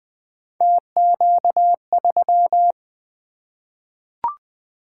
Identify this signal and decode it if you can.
Morse code, heard as a keyed tone, about 20 wpm, 700 Hz. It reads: TQ3
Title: Three Letter-Number Combinations - 20wpm